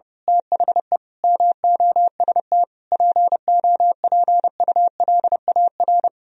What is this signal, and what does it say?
Morse code, heard as a keyed tone, about 28 wpm, 700 Hz. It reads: THE MOST POPULAR
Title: Speed-Racing - Sets of 5 Words 30wpm